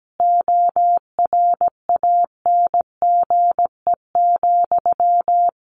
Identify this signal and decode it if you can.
Morse code, heard as a keyed tone, about 17 wpm, 700 Hz. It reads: ORANGE,
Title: A Sound Of Thunder 17wpm